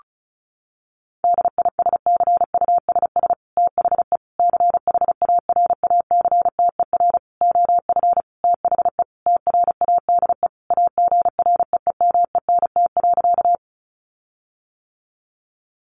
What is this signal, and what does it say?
Morse code, heard as a keyed tone, about 34 wpm, 700 Hz. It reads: DISCUSS THE CHARACTER OF THE TRADE AGREEMENT.
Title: Sentences from Top 1500 Words 35wpm